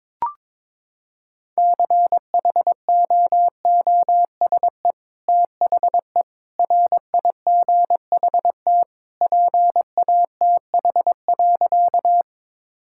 Morse code, heard as a keyed tone, about 22 wpm, 700 Hz. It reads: CHOOSE THE RIGHT PATH.
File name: practice / 22wpm / Sentences with 2-Word Phrases - 22wpm